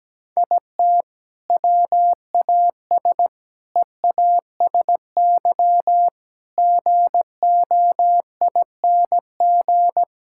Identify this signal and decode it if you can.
Morse code, heard as a keyed tone, about 17 wpm, 700 Hz. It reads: IT WAS EASY GOING
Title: Sentences from Top 1000 Words - Review Part 5 - 17wpm